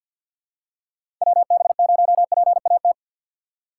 Morse code, heard as a keyed tone, about 46 wpm, 700 Hz. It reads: WB0PAT
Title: USA Call Signs - 6 Characters - 3x Character Spacing - 50wpm